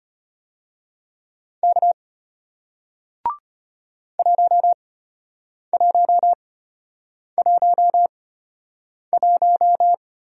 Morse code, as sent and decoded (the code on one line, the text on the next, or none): -.- / .---- .---- .---- .----
K 1111